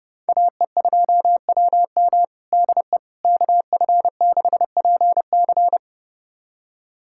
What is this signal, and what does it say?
Morse code, heard as a keyed tone, about 32 wpm, 700 Hz. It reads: AE2WM DE KF6PC
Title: USA Callsign DE Callsign - 5 Characters - 30wpm